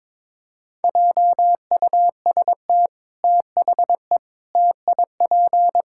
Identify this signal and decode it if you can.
Morse code, heard as a keyed tone, about 22 wpm, 700 Hz. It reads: JUST THE TIP
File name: practice / 22wpm / Sets of 6 Words - 22wpm